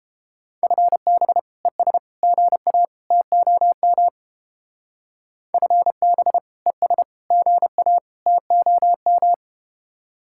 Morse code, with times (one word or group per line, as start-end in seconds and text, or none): 0.6-1.4 FB
1.6-2.0 ES
2.2-2.9 GA
3.1-4.1 TOM
5.5-6.4 FB
6.7-7.0 ES
7.3-8.0 GA
8.3-9.3 TOM